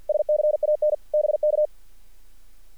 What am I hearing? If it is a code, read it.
Morse code, heard as a keyed tone, about 50 wpm, 600 Hz. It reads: SCAN BK
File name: archived / CRSnet